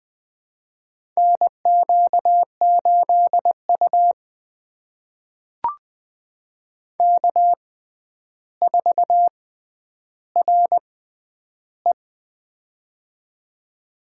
Morse code, heard as a keyed tone, about 20 wpm, spaced as for 10 wpm, 700 Hz. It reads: NQ8U K4RE
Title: USA Call Signs - 4 Characters - 6x Character Spacing - 20wpm